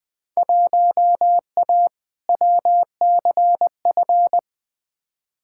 Morse code, heard as a keyed tone, about 20 wpm, 700 Hz. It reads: 1A WCF
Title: Field Day Exchanges - 20wpm